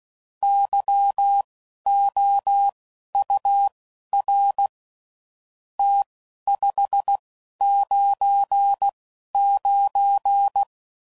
Morse code, heard as a keyed tone, about 16 wpm, spaced as for 11 wpm, 800 Hz. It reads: YOUR T 599
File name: CW Code Groups @10WPM / QSOs / QSO WA7QOX